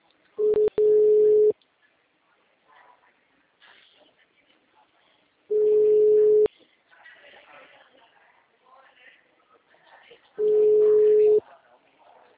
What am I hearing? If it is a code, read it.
Ringback tone (Europe, 425 Hz), with speech.